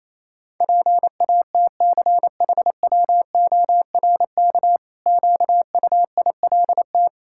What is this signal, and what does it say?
Morse code, heard as a keyed tone, about 28 wpm, 700 Hz. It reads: PATCHWORK QUILT
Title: Pangram Challenge v2 - 28wpm